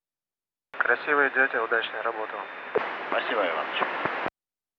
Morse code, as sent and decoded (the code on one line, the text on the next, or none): none